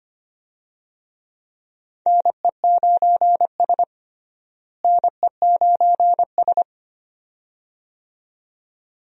Morse code, heard as a keyed tone, about 25 wpm, 700 Hz. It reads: NE9S NE9S
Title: USA Call Signs - 4 Characters - Repeated 2x - 25wpm